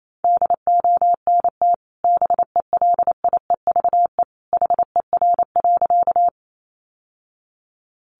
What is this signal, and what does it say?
Morse code, heard as a keyed tone, about 28 wpm, 700 Hz. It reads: DONT BELIEVE HER.
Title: Sentences from Top 1000 Words 28wpm